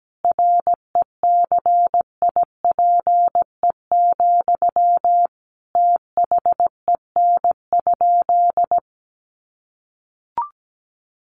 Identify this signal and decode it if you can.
Morse code, heard as a keyed tone, about 17 wpm, 700 Hz. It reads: RECIPE, THEN?
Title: Sentences from Top 1200 Words 17wpm